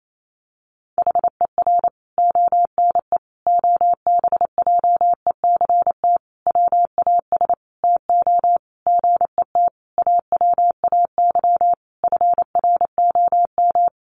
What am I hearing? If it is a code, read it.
Morse code, heard as a keyed tone, about 28 wpm, 700 Hz. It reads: HER ONE OBJECT WAS TO GET AWAY FROM
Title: Sentences from Top 1500 Words 28wpm